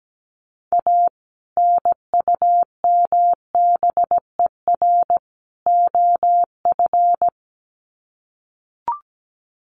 Morse code, heard as a keyed tone, about 17 wpm, 700 Hz. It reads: A NUMBER OF
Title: Sets of 3 Words 17wpm